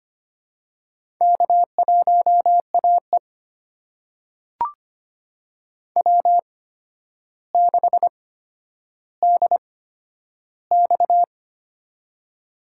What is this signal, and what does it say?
Morse code, heard as a keyed tone, about 25 wpm, spaced as for 11 wpm, 700 Hz. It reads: K1AE W6DX
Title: USA Call Signs - 4 Characters - 8x Character Spacing - 25wpm